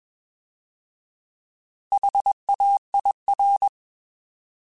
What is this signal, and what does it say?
Morse code, heard as a keyed tone, about 21 wpm, 800 Hz. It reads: HAIR